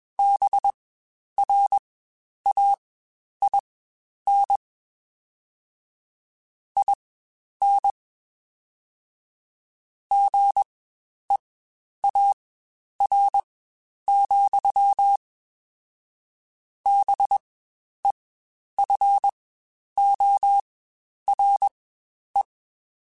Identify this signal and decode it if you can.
Morse code, heard as a keyed tone, about 21 wpm, spaced as for 10 wpm, 800 Hz. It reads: BRAIN IN GEAR, BEFORE
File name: Spaced / Chuckles Spaced-mp3s / BrainMouth sp @20